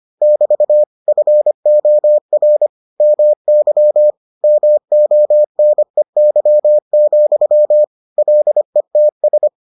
Morse code, heard as a keyed tone, about 25 wpm, 600 Hz. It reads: <BT> FOR MY MONEY, LETS